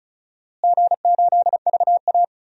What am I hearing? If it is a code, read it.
Morse code, heard as a keyed tone, about 34 wpm, 700 Hz. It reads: G8VA
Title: DXCC-100 Call Signs - 4 Characters - Repeated 2x - 35wpm